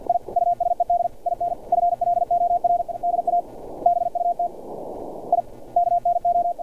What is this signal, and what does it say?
Morse code, heard as a keyed tone, about 50 wpm, 700 Hz. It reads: I USED IN FLOR DA BUT I KNO
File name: archived / CRSnet